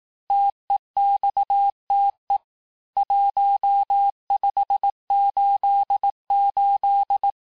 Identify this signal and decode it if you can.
Morse code, heard as a keyed tone, about 18 wpm, 800 Hz. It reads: TEXTE 1588
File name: cours-du-lundi files / cours-1588-texte-chiffre-vitesse-17MPM